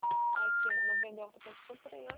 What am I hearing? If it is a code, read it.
Special information tone, with speech.